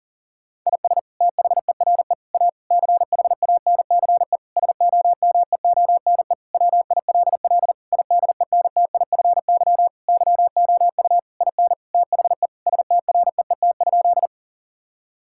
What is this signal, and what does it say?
Morse code, heard as a keyed tone, about 37 wpm, 700 Hz. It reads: IS THERE A CHANCE SOMEONE WILL IDENTIFY YOU IN THE STREET?
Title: Sentences from Top 1000 Words - Review Part 6 - 40wpm